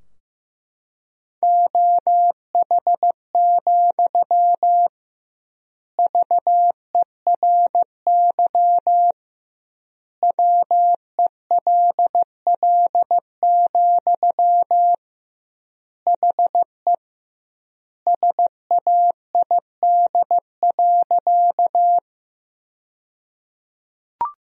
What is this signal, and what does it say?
Morse code, heard as a keyed tone, about 16 wpm, 700 Hz. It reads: OH, VERY WELL, HE SAID.
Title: 2x Word-Spacing - Sentences from Top 500 Words - 15wpm